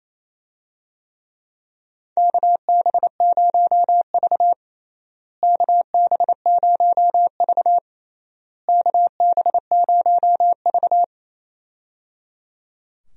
Morse code, heard as a keyed tone, about 28 wpm, 700 Hz. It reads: KB0V KB0V KB0V
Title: USA Call Signs - 4 Characters - Repeated 3x - 28wpm